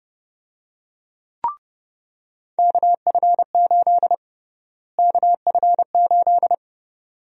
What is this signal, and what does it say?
Morse code, heard as a keyed tone, about 31 wpm, 700 Hz. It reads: KF8 KF8
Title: USA Call Sign Prefixes - Repeated 2x - 30wpm